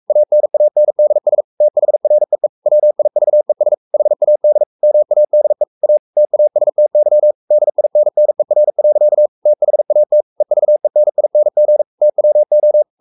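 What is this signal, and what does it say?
Morse code, heard as a keyed tone, about 44 wpm, 600 Hz. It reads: ANANDS THREE WIVES HAD MADE A TASTY DINNER. THAT EVENING TWO
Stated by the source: Morse code at 43 wpm sending Ken Follett's novel Never in English, from CW med Gnister.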